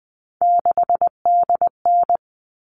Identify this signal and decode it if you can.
Morse code, heard as a keyed tone, about 20 wpm, 700 Hz. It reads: 6DN
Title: All Cut Numbers - 3-Digit 20wpm